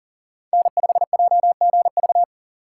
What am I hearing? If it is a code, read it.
Morse code, heard as a keyed tone, about 37 wpm, 700 Hz. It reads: N5JGV
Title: USA Call Signs - 5 Characters - 4x Character Spacing - 40wpm